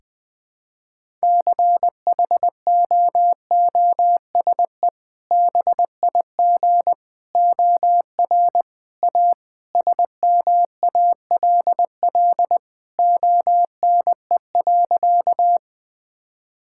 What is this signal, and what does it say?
Morse code, heard as a keyed tone, about 20 wpm, 700 Hz. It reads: CHOOSE BIG OR A SMALL ONE.
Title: Sentences with 2-Word Phrases - 20wpm